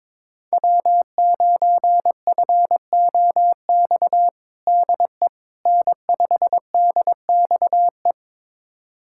Morse code, heard as a keyed tone, about 22 wpm, 700 Hz. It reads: W9FOX DE N5DXE
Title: USA Callsign DE Callsign - 5 Characters - 22wpm